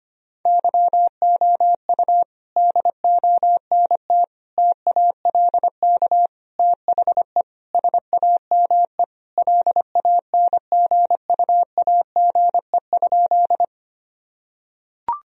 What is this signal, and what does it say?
Morse code, heard as a keyed tone, about 25 wpm, 700 Hz. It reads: YOU DONT TALK THE SAME LANGUAGE?
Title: Sentences from Top 1000 Words - Review Part 3 - 25wpm